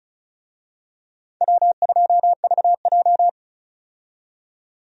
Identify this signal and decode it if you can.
Morse code, heard as a keyed tone, about 36 wpm, 700 Hz. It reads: W2VJ
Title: USA Call Signs - 4 Characters - 35wpm